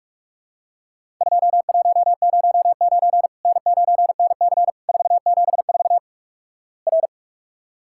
Morse code, heard as a keyed tone, about 43 wpm, 700 Hz. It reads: 2109 N9NC 474 R
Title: Worked All Europe DX Contest - QTC Exchange - 45wpm